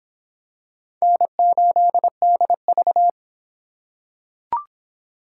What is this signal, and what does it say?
Morse code, heard as a keyed tone, about 23 wpm, 700 Hz. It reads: N8DV